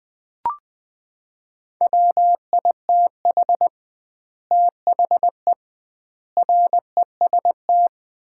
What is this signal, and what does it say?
Morse code, heard as a keyed tone, about 20 wpm, 700 Hz. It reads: WITH THE REST
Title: 2x Word-Spacing - Sets of 5 Words - 20wpm